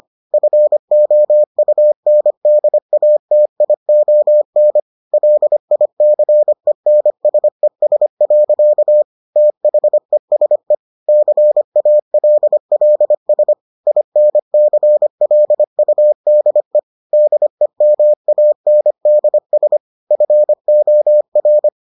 Morse code, heard as a keyed tone, about 25 wpm, 600 Hz. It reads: FOUNDATION LICENSES. THESE CALLS INCLUDE DEMANDS FOR